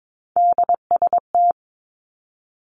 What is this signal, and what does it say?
Morse code, heard as a keyed tone, about 22 wpm, 700 Hz. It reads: DST
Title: Three Challenging Letter-Number Combinations 22wpm